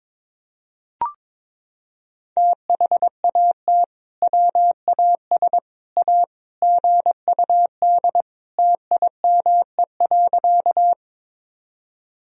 Morse code, heard as a keyed tone, about 22 wpm, 700 Hz. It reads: THAT WAS A GUD TIME.
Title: Sentences from Top 200 Words and Abbreviations 22wpm